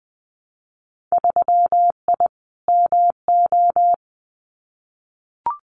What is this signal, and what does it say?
Morse code, heard as a keyed tone, about 20 wpm, 700 Hz. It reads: 3I MO
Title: Winter Field Day Exchanges 20wpm